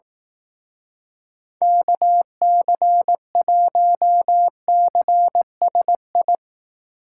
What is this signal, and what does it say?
Morse code, heard as a keyed tone, about 18 wpm, 700 Hz. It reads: KC1CSI
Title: Speed Racing - USA 2x3 Call Signs 15wpm